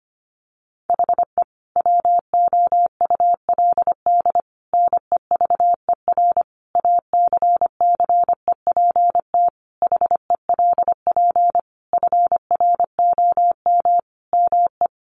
Morse code, heard as a keyed tone, about 25 wpm, 700 Hz. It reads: HE WOULD NEVER ACCEPT HELP FROM ME
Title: Sentences from Top 1300 Words 25wpm